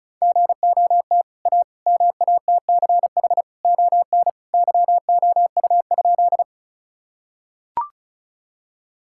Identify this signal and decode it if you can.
Morse code, heard as a keyed tone, about 34 wpm, 700 Hz. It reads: GOT A MATCH ON YOU?
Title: Sentences from Top 1600 Words 35wpm